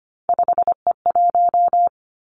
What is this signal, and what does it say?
Morse code, heard as a keyed tone, about 25 wpm, 700 Hz. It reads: 5E1
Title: Three Challenging Letter-Number Combinations 25wpm